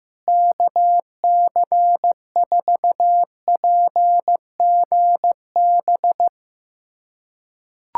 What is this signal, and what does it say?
Morse code, heard as a keyed tone, about 15 wpm, 700 Hz. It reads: KC4PGB
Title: USA Callsign DE Callsign 15wpm